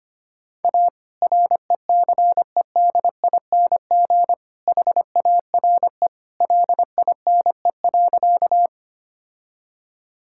Morse code, heard as a keyed tone, about 25 wpm, 700 Hz. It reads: A RECEDING HARE LINE.
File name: practice / 25wpm / All-Age Appropriate Jokes - 25wpm